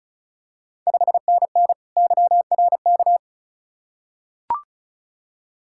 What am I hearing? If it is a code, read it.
Morse code, heard as a keyed tone, about 36 wpm, 700 Hz. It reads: 5NN YRK
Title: Ontario QSO Party Exchange 35wpm